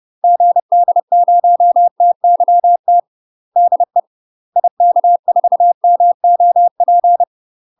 Morse code, heard as a keyed tone, about 30 wpm, 700 Hz. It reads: GD0TYT DE IK4MOP
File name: UK MASTER QSOs TRAINING / 30WPM / UK MASTER QSOs 0033